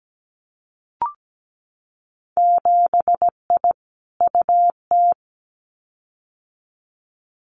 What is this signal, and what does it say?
Morse code, heard as a keyed tone, about 17 wpm, 700 Hz. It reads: 7I UT
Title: Winter Field Day Exchanges Encore 17wpm